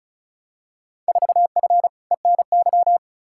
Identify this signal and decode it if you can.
Morse code, heard as a keyed tone, about 36 wpm, 700 Hz. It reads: 4F ENY
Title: Field Day Exchanges - 35wpm